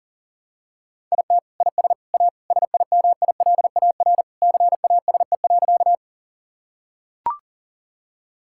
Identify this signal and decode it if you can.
Morse code, heard as a keyed tone, about 40 wpm, 700 Hz. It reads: IT IS A SIMILAR CASE.
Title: Sentences from Top 1000 Words - Review Part 5 - 40wpm